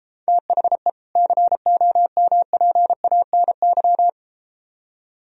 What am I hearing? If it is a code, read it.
Morse code, heard as a keyed tone, about 33 wpm, 700 Hz. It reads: THE COMPANY